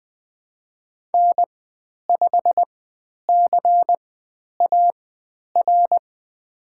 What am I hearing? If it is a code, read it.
Morse code, heard as a keyed tone, about 20 wpm, spaced as for 10 wpm, 700 Hz. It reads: N5CAR
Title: 109 Lesson 18g - Letter m - Call Signs 20wpm